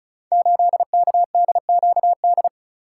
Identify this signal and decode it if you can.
Morse code, heard as a keyed tone, about 36 wpm, 700 Hz. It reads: 8KDQD
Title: Code Groups - 5 Letter:Numbers - 35wpm